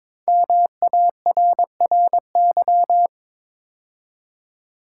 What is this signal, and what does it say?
Morse code, heard as a keyed tone, about 22 wpm, 700 Hz. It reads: MARRY
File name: 5 Letter Words - First 3 Letters Given - 22wpm